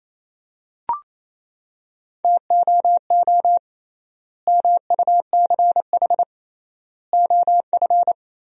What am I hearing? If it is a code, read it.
Morse code, heard as a keyed tone, about 27 wpm, 700 Hz. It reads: TOO MUCH OF
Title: 3x Word-Spacing - Sentences from Top 200 Words - 28wpm